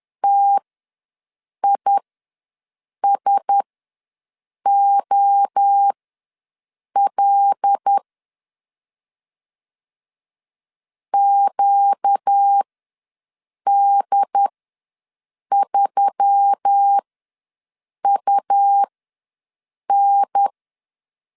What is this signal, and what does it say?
Morse code, heard as a keyed tone, about 10 wpm, spaced as for 6 wpm, 800 Hz. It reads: TISOL QD3UN
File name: LEZIONI / 5 LBVC / 5 L ws 10wpm 3636